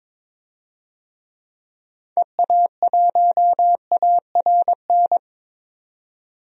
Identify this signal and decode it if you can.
Morse code, heard as a keyed tone, about 22 wpm, 700 Hz. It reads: EA1ARN
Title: DXCC-100 Call Signs - 6 Characters - 22wpm